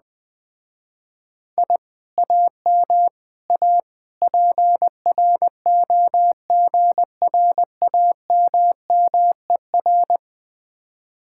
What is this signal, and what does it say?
Morse code, heard as a keyed tone, about 20 wpm, 700 Hz. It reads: I AM A PROGRAMMER
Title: Speed-Racing Common QSO Phrases Encore - 17wpm